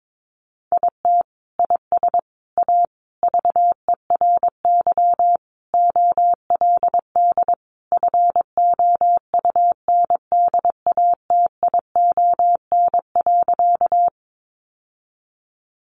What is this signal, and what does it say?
Morse code, heard as a keyed tone, about 22 wpm, 700 Hz. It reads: IT IS A VERY OLD FOUNDATION.
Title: Sentences from Top 1200 Words 22wpm